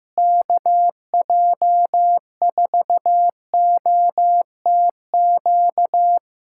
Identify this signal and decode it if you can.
Morse code, heard as a keyed tone, about 15 wpm, 700 Hz. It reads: KJ4OTQ
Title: Practice QSO - Exchange 4 - Ending - 15wpm